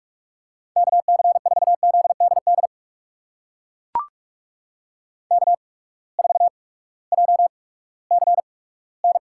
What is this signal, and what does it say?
Morse code, heard as a keyed tone, about 44 wpm, spaced as for 20 wpm, 700 Hz. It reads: KC4ZDD K4JCN
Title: USA Call Signs - 8x Character Spacing - 45wpm